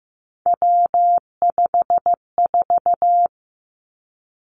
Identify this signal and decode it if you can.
Morse code, heard as a keyed tone, about 15 wpm, 700 Hz. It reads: W54
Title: Three Challenging Letter-Number Combinations 15wpm